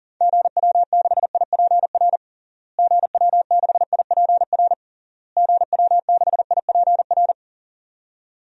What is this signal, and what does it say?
Morse code, heard as a keyed tone, about 40 wpm, 700 Hz. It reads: GW6IPR GW6IPR GW6IPR
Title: DXCC-100 Call Signs - 6 Characters - Repeated 3x - 40wpm